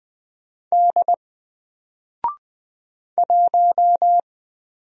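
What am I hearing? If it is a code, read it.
Morse code, heard as a keyed tone, about 20 wpm, 700 Hz. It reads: D 1
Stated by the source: Single letter and number Morse drill at 20 wpm, from N4LCD.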